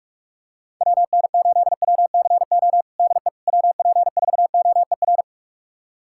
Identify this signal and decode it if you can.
Morse code, heard as a keyed tone, about 46 wpm, 700 Hz. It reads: WN8WCO DE WP4OER
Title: USA Callsign DE Callsign - 6 Characters - 45wpm